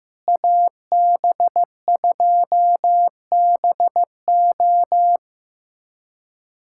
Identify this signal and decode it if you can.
Morse code, heard as a keyed tone, about 15 wpm, 700 Hz. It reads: AB2BO
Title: USA Callsign DE Callsign - 5 Characters - 15wpm